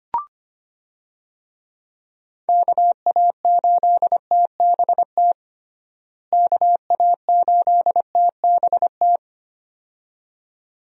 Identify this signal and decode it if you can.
Morse code, heard as a keyed tone, about 25 wpm, 700 Hz. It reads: KA8TBT KA8TBT
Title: USA Call Signs - 6 Characters - Repeated 2x - 25wpm